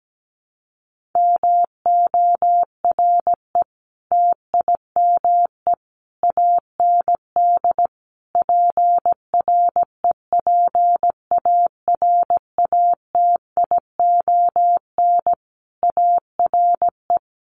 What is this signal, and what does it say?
Morse code, heard as a keyed tone, about 17 wpm, 700 Hz. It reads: MORE TIME AND PREPARATION ARE
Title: Sentences from Top 1400 Words 17wpm